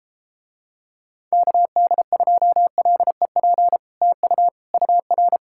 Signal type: Morse code

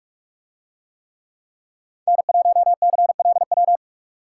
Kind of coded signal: Morse code

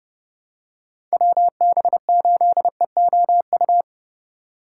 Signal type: Morse code

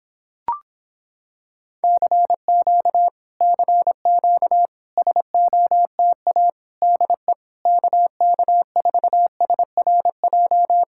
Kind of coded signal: Morse code

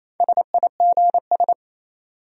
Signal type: Morse code

